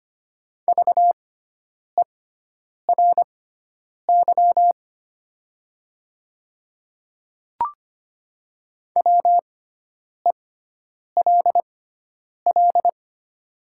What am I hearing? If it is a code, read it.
Morse code, heard as a keyed tone, about 25 wpm, spaced as for 9 wpm, 700 Hz. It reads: VERY WELL
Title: 6x Character Spacing - Top 200 Words - 25wpm